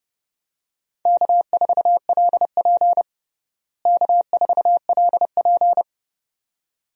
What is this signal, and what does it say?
Morse code, heard as a keyed tone, about 30 wpm, 700 Hz. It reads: K4LP K4LP
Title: USA Call Signs - 4 Characters - Repeated 2x - 30wpm